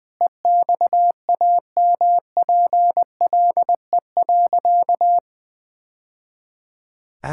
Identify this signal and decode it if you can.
Morse code, heard as a keyed tone, about 21 wpm, 700 Hz. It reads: EXAMPLE.
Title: Sentences with 2-Word Phrases - 20wpm